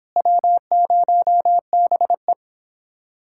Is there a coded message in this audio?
Yes: Morse code from 0.2 to 2.3 s, heard as a keyed tone, about 29 wpm, 700 Hz. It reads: W0BE